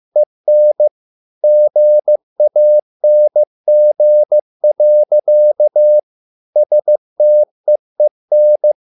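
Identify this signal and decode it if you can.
Morse code, heard as a keyed tone, about 15 wpm, 600 Hz. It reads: EN GANG. STEEN
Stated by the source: Morse code at a slow 15 wpm from CW med Gnister.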